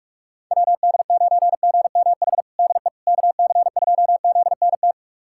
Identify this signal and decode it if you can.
Morse code, heard as a keyed tone, about 46 wpm, 700 Hz. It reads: WD9GMH DE KC2ZNT
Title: USA Callsign DE Callsign 45wpm